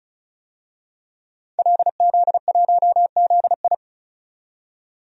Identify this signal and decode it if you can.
Morse code, heard as a keyed tone, about 34 wpm, 700 Hz. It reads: LZ1ZI
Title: DXCC-100 Call Signs - 5 Characters - 35wpm